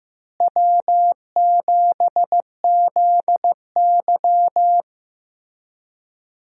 Morse code, heard as a keyed tone, about 15 wpm, 700 Hz. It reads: W7ZY
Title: USA Call Signs - 4 Characters - Repeated 3x - 15wpm